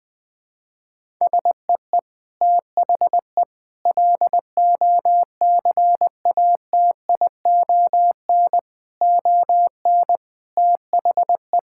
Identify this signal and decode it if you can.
Morse code, heard as a keyed tone, about 21 wpm, 700 Hz. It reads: SEE THE LOCATION ON THE
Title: Sentences from Top 1000 Words - Review Part 6 - 20wpm